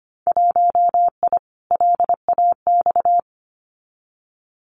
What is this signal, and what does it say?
Morse code, heard as a keyed tone, about 25 wpm, 690 Hz. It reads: 1I LAX
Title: Winter Field Day Exchanges 25wpm